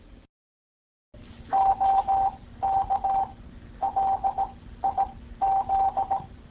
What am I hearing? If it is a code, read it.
Morse code, heard as a keyed tone, about 18 wpm, spaced as for 13 wpm, 780 Hz. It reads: OK LIZ